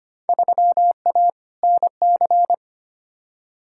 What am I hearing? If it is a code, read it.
Morse code, heard as a keyed tone, about 25 wpm, 700 Hz. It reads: 3A NC
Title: Field Day Exchanges - 25wpm